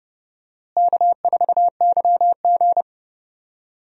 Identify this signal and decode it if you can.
Morse code, heard as a keyed tone, about 32 wpm, 700 Hz. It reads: K4YG
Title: USA Call Signs - 4 Characters - Repeated 2x - 30wpm